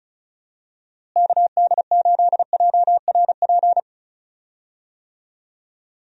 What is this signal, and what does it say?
Morse code, heard as a keyed tone, about 36 wpm, 700 Hz. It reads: KD8JRP
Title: USA Call Signs - 6 Characters - Repeated 3x - 35wpm